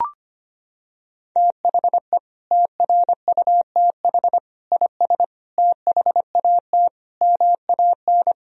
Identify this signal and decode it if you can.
Morse code, heard as a keyed tone, about 25 wpm, 700 Hz. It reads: THE TRUTH IS THAT MAN
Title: Sentences from Top 1000 Words - Review Part 1 - 25wpm